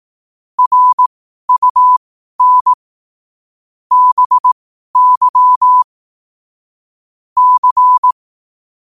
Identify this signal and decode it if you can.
Morse code, heard as a keyed tone, about 18 wpm, spaced as for 12 wpm, 1000 Hz. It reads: RUN BY C
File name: qotdfiles / 12wpm